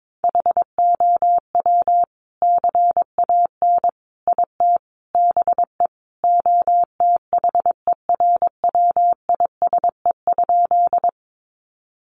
Morse code, heard as a keyed tone, about 22 wpm, 700 Hz. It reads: HOW CAN IT BE OTHERWISE?
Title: Sentences from Top 1200 Words 22wpm